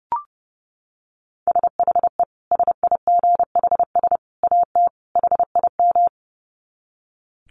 Morse code, heard as a keyed tone, about 30 wpm, 700 Hz. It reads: SHE SIGHS AT HIM
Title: Dit Challenge 30wpm